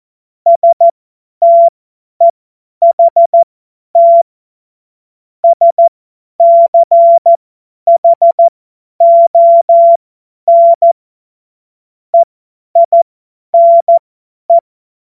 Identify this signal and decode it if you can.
Morse code, heard as a keyed tone, about 14 wpm, spaced as for 10 wpm, 680 Hz. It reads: STEHT SCHON EINE
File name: Morsekurs / Witze / Witz 20 Februar 2024 50 F70